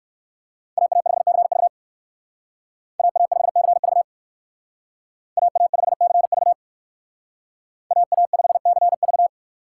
Morse code, heard as a keyed tone, about 58 wpm, 700 Hz. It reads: AA5CV AA5CV AA5CV AA5CV